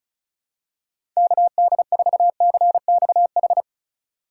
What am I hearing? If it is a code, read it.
Morse code, heard as a keyed tone, about 36 wpm, 700 Hz. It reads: KD4CXH
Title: USA Call Signs - 6 Characters - 3x Character Spacing - 35wpm